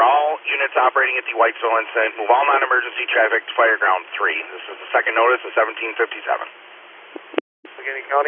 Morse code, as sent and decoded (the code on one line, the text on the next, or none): none